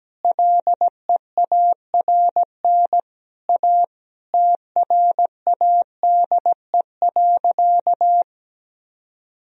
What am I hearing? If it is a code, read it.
Morse code, heard as a keyed tone, about 17 wpm, 700 Hz. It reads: LEARN A TRADE.
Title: Sentences from Top 1000 Words - Review Part 4 - 17wpm